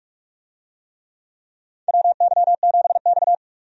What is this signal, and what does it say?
Morse code, heard as a keyed tone, about 46 wpm, 700 Hz. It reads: WY7X
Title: USA Call Signs - 4 Characters - 45wpm